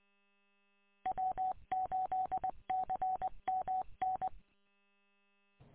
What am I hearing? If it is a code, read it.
Morse code, heard as a keyed tone, about 24 wpm, 750 Hz. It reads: W8CMN
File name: w8cmn- -combined